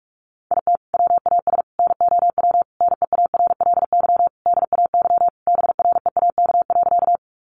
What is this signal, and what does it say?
Morse code, heard as a keyed tone, about 44 wpm, 700 Hz. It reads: IT WAS NOW NEARLY DAY BREAK.
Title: Sentences from Top 1300 Words 45wpm